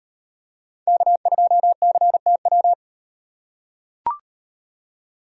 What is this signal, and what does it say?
Morse code, heard as a keyed tone, about 38 wpm, 700 Hz. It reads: K2CTW